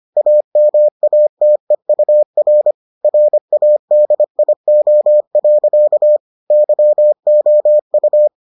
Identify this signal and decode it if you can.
Morse code, heard as a keyed tone, about 25 wpm, 600 Hz. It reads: AMATEUR RADIO. YOU